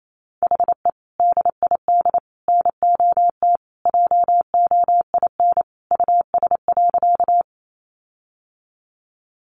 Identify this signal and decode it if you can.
Morse code, heard as a keyed tone, about 28 wpm, 700 Hz. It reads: HE DID NOT JOIN US.
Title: Sentences from Top 1400 Words 28wpm